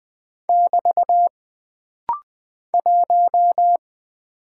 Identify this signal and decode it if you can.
Morse code, heard as a keyed tone, about 20 wpm, 700 Hz. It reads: <BT> 1
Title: Single Character (Letter:Number:Punctuation:Prosign) - Rapid-Fire - 20wpm